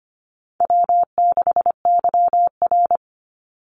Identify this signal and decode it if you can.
Morse code, heard as a keyed tone, about 25 wpm, 700 Hz. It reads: W6YR
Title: Easy Call Signs Repeated 3x - 25wpm